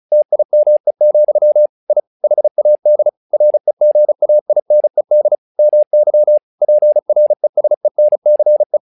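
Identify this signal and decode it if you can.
Morse code, heard as a keyed tone, about 34 wpm, 600 Hz. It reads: TIME, I HAD REGAINED MY PRESENCE
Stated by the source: Morse practice text "Tower Trouble" sent at 35 WPM, from the SKCC - Straight Key Century Club learning center audio files.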